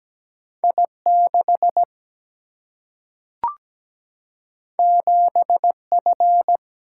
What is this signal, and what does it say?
Morse code, heard as a keyed tone, about 17 wpm, 700 Hz. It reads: I6 7F
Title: Single-Letter Single-Number 17wpm